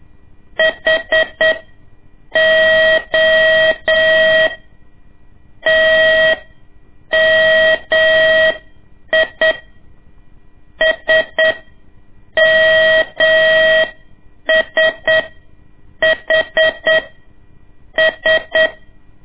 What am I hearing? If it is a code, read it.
Morse code, heard as a keyed tone, about 8 wpm, spaced as for 6 wpm, 640 Hz. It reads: HOTMI SMSHS
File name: ik1weg2 / CORSI CW / lez2-1